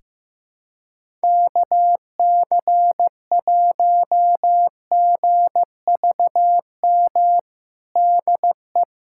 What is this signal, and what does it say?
Morse code, heard as a keyed tone, about 16 wpm, 700 Hz. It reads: KC1GVM DE
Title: USA Callsign DE Callsign 15wpm